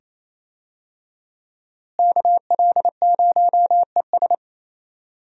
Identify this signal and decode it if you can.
Morse code, heard as a keyed tone, about 28 wpm, 700 Hz. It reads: KL0ES
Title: USA Call Signs - 5 Characters - Repeated 2x - 28wpm